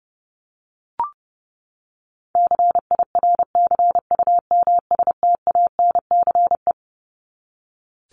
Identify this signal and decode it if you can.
Morse code, heard as a keyed tone, about 30 wpm, 700 Hz. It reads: CIRCUMSTANCE
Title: ES:S Suffix - Top 100 Words with Base Words 30wpm